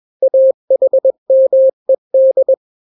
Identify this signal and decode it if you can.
Morse code, heard as a keyed tone, about 21 wpm, 530 Hz. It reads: AHMED
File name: daily / 20wpm